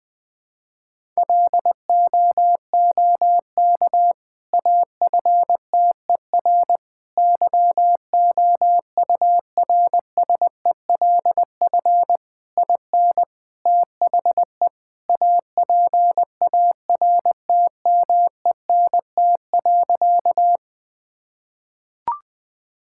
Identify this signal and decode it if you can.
Morse code, heard as a keyed tone, about 21 wpm, 700 Hz. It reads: LOOK AFTER YOURSELF IN THE APARTMENT.
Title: Sentences from Top 1000 Words - Review Part 6 - 20wpm